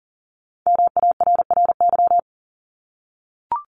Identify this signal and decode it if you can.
Morse code, heard as a keyed tone, about 38 wpm, 700 Hz. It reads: MARRY
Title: Multi-letter Phonemes 40wpm